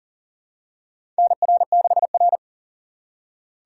Morse code, heard as a keyed tone, about 40 wpm, 700 Hz. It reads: NR6R
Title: USA Call Signs - 4 Characters - 2x Character Spacing - 40wpm